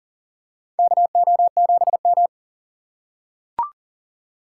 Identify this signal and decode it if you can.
Morse code, heard as a keyed tone, about 40 wpm, 700 Hz. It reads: KO7M
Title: USA Call Signs - 4 Characters - 3x Character Spacing - 40wpm